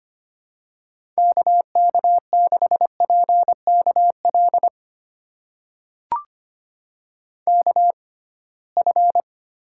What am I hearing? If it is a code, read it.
Morse code, heard as a keyed tone, about 25 wpm, spaced as for 19 wpm, 700 Hz. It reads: KK6PKL KF
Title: USA Call Signs - 6 Characters - 6x Character Spacing - 25wpm